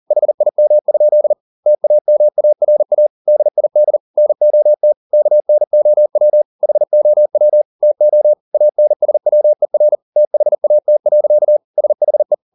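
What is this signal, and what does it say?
Morse code, heard as a keyed tone, about 40 wpm, 600 Hz. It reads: HIM? TAMARA DID NOT KNOW HOW TO ANSWER THAT. SHE
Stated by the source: Morse code at 40 wpm, practice text, CW med Gnister.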